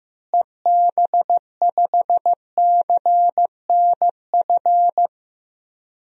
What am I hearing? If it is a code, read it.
Morse code, heard as a keyed tone, about 15 wpm, 700 Hz. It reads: EB5CNF
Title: DXCC-100 Call Signs - 6 Characters - Repeated 2x - 15wpm